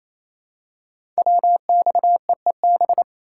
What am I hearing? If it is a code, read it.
Morse code, heard as a keyed tone, about 29 wpm, 700 Hz. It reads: WXEEB
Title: Code Groups - 5 Letters - 28wpm